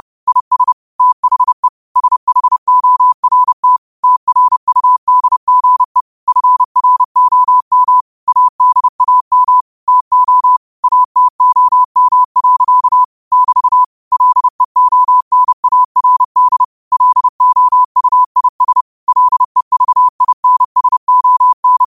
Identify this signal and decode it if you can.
Morse code, heard as a keyed tone, about 30 wpm, 1000 Hz. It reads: IS THE SHORT TRUDGE FROM ADAM TO ATOM. <BT> LEONARD LOUIS LEVINSON
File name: qotdfiles / 30wpm